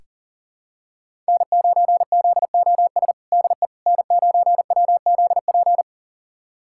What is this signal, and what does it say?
Morse code, heard as a keyed tone, about 40 wpm, 700 Hz. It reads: N9ZOS DE N9WZP
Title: USA Callsign DE Callsign - 5 Characters - 40wpm